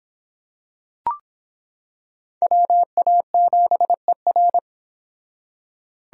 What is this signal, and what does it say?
Morse code, heard as a keyed tone, about 26 wpm, 700 Hz. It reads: WA7ER